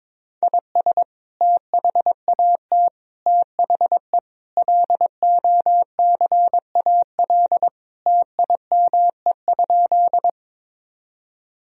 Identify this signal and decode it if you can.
Morse code, heard as a keyed tone, about 21 wpm, 700 Hz. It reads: IS THAT THE LOCAL TIME?
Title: Sentences from Top 1000 Words - Review Part 3 - 22wpm